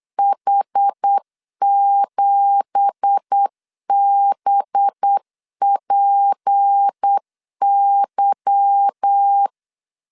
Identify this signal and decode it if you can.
Morse code, heard as a keyed tone, about 8 wpm, 800 Hz. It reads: H7BPY
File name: LEZIONI / 10 SIGNS / 10A SLASH 8wpm 3452